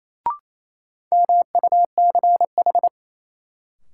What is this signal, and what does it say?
Morse code, heard as a keyed tone, about 28 wpm, 700 Hz. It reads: MUCH
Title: Sentences from Top 200 Words - New Word Drill - 28wpm